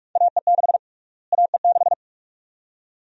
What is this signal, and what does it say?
Morse code, heard as a keyed tone, about 47 wpm, 700 Hz. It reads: AE6 AE6
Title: USA Call Sign Prefixes - Repeated 2x - 45wpm